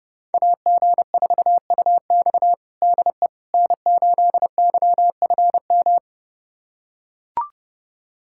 Morse code, heard as a keyed tone, about 30 wpm, 700 Hz. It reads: AG4UX DE N8YFM
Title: USA Callsign DE Callsign - 5 Characters - 30wpm